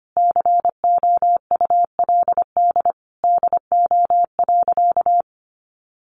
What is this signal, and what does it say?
Morse code, heard as a keyed tone, about 25 wpm, 700 Hz. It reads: COULD DO.
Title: Sentences from Top 1400 Words 25wpm